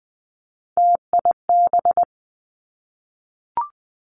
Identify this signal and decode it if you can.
Morse code, heard as a keyed tone, about 20 wpm, 700 Hz. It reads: TIB
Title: Three Challenging Letter-Number Combinations 20wpm